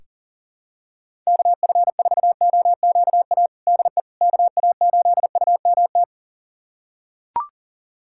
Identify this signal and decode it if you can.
Morse code, heard as a keyed tone, about 40 wpm, 700 Hz. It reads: KF4OQA DE KA8UMT
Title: USA Callsign DE Callsign - 6 Characters - 40wpm